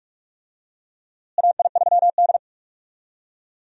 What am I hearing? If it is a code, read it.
Morse code, heard as a keyed tone, about 46 wpm, 700 Hz. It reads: AI3D
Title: USA Call Signs - 4 Characters - Repeated 3x - 45wpm